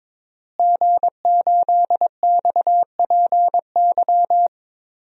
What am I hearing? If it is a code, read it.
Morse code, heard as a keyed tone, about 22 wpm, 700 Hz. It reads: G8XPY
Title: DXCC-100 Call Signs - 5 Characters - Repeated 3x - 22wpm